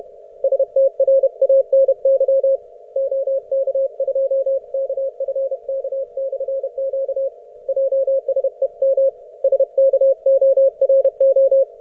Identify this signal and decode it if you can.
Morse code, heard as a keyed tone, about 31 wpm, 540 Hz. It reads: STRANY OK2KFK/Q JSEM SKORO